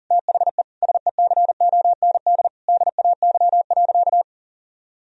Morse code, heard as a keyed tone, about 40 wpm, 700 Hz. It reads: THE SECOND DAY.
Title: Sentences from Top 1000 Words - Review Part 4 - 40wpm